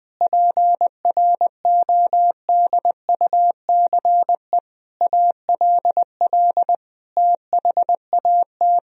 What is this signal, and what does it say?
Morse code, heard as a keyed tone, about 20 wpm, 700 Hz. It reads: PRODUCE ALL THAT
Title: Sentences from Top 1000 Words - Review Part 3 - 20wpm